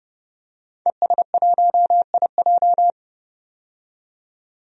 Morse code, heard as a keyed tone, about 30 wpm, 700 Hz. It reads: ES1IJ
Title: DXCC-100 Call Signs - 5 Characters - 30wpm